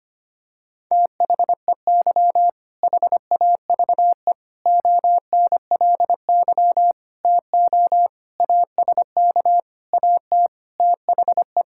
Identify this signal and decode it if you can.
Morse code, heard as a keyed tone, about 25 wpm, 700 Hz. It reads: THEY HAVE ONLY TO ASK AT THE
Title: Sentences from Top 1000 Words - Review Part 5 - 25wpm